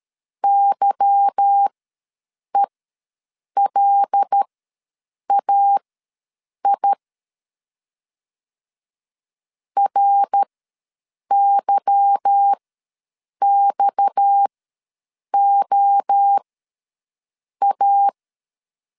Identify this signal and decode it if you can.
Morse code, heard as a keyed tone, about 13 wpm, spaced as for 7 wpm, 800 Hz. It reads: YELAI RYXOA
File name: LEZIONI / 8 YJ45 / 8 Y ws 12wpm 3409